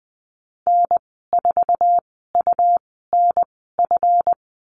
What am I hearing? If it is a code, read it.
Morse code, heard as a keyed tone, about 20 wpm, spaced as for 15 wpm, 700 Hz. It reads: N4UNF
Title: 186 Lesson 30g - Letter b - Call Signs 20wpm